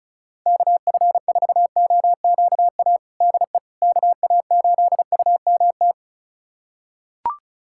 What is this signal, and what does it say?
Morse code, heard as a keyed tone, about 34 wpm, 700 Hz. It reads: KF4OQA DE KA8UMT
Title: USA Callsign DE Callsign - 6 Characters - 35wpm